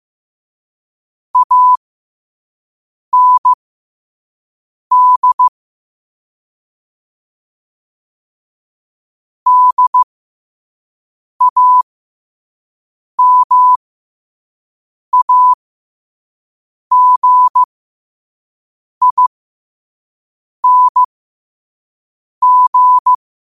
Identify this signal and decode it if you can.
Morse code, heard as a keyed tone, about 15 wpm, spaced as for 5 wpm, 1000 Hz. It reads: AND DAMAGING
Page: New quotes every day in morse code at 5 Words per minute.